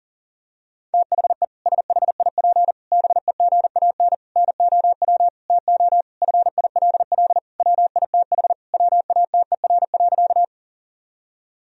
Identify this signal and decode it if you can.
Morse code, heard as a keyed tone, about 37 wpm, 700 Hz. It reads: THE SHIP BEGAN NOW TO FILL WITH WATER.
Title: Sentences from Top 1000 Words - Review Part 3 - 40wpm